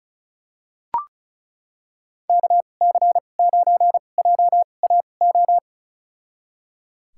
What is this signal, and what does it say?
Morse code, heard as a keyed tone, about 34 wpm, spaced as for 25 wpm, 700 Hz. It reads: KC9JAO
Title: USA Call Signs - 6 Characters - 2x Character Spacing - 35wpm